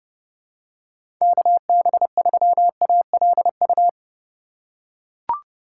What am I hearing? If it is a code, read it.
Morse code, heard as a keyed tone, about 30 wpm, 700 Hz. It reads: KB3ALU
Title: USA Call Signs - 6 Characters - 4x Character Spacing - 30wpm